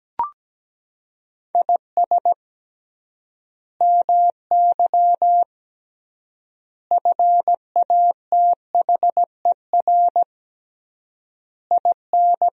Morse code, heard as a keyed tone, about 17 wpm, 700 Hz. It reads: IS MY FATHER IN
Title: 3x Word-Spacing - Sentences from Top 500 Words - 17wpm